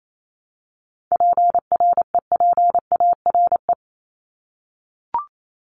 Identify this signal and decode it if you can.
Morse code, heard as a keyed tone, about 28 wpm, 700 Hz. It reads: PREPARE
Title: ED Suffix - Top 100 Words with Base Words 28wpm